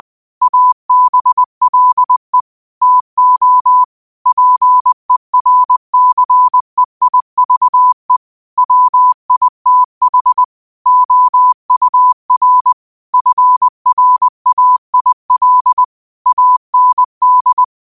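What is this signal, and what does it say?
Morse code, heard as a keyed tone, about 20 wpm, 1000 Hz. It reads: ABLE TO PERCEIVE WITH OUR FRAIL AND
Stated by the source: Morse quote of the day sent at 20 wpm.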